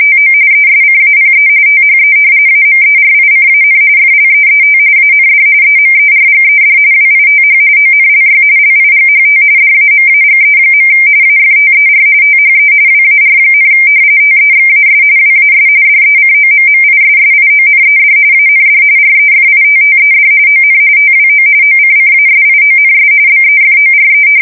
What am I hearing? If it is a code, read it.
RTTY, about 45.45 baud. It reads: TO ESCAPE THE AMERICAN FLEET AND EMERGED WITH ONLY LIGHT DAMAGE. AFTER THE WAR, TEXAS WAS DECOMMISSIONED AND REFITTED ON TWO OCCASIONS BEFORE
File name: fsk170 / battleships